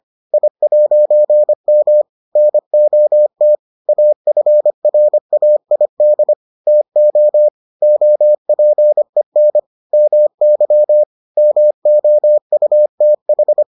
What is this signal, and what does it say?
Morse code, heard as a keyed tone, about 25 wpm, 600 Hz. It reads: I'M NOT AFRAID TO OPEN MY MOUTH